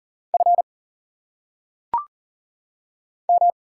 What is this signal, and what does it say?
Morse code, heard as a keyed tone, about 38 wpm, 700 Hz. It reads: F M
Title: Single Letters - Phonetic Pronunciation 40wpm